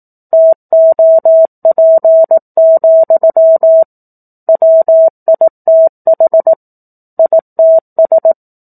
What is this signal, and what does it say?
Morse code, heard as a keyed tone, about 18 wpm, 650 Hz. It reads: TOP, WITH ITS
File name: 04-Chapter-04 18 WPM